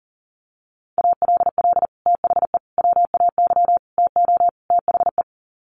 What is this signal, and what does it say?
Morse code, heard as a keyed tone, about 40 wpm, 700 Hz. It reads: ALL THE WAY TO THE
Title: Sets of 5 Words 40wpm